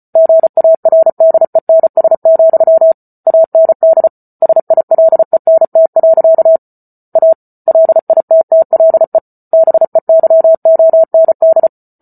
Morse code, heard as a keyed tone, about 34 wpm, 650 Hz. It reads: GARDENS, AND SILENT. A LITTLE BEYOND
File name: Book Two / 08-Chapter 8 - 34 WPM